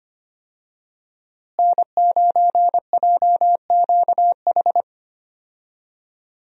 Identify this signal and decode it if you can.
Morse code, heard as a keyed tone, about 25 wpm, 700 Hz. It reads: N9JQH
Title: USA Call Signs - 5 Characters - Repeated 2x - 25wpm